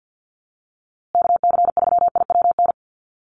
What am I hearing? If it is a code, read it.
Morse code, heard as a keyed tone, about 50 wpm, 700 Hz. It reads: KC3IWN
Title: USA Call Signs Repeated 2x 50wpm